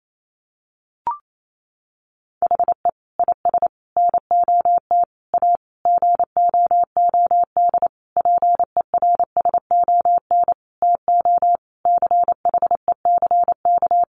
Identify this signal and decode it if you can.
Morse code, heard as a keyed tone, about 28 wpm, 700 Hz. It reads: HE IS NOT A GOOD PERSON TO CHECK
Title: Sentences from Top 1400 Words 28wpm